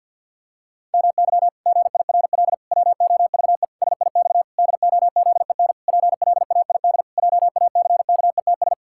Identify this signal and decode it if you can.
Morse code, heard as a keyed tone, about 46 wpm, 700 Hz. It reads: MY GIRL WOVE SIX DOZEN PLAID JACKETS
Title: Pangram Challenge v2 - 50wpm